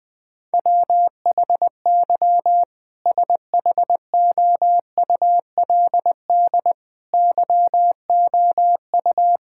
Morse code, heard as a keyed tone, about 19 wpm, 700 Hz. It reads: WHY SHOULD YOU
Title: Sentences from Top 1600 Words 20wpm